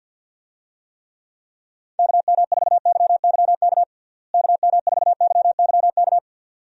Morse code, heard as a keyed tone, about 53 wpm, 700 Hz. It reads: KM4YYK KM4YYK
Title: USA Call Signs - 6 Characters - Repeated 2x - 50wpm